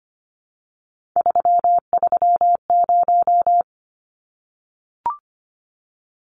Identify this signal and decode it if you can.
Morse code, heard as a keyed tone, about 25 wpm, 700 Hz. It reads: 330
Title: Numbers - 3-Digit 25wpm